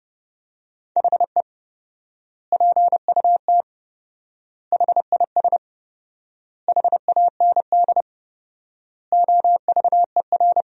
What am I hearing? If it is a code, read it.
Morse code, heard as a keyed tone, about 30 wpm, 700 Hz. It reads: HE PUT HIS HAND OVER
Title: 4x Word-Spacing - Sentences from Top 400 Words - 30wpm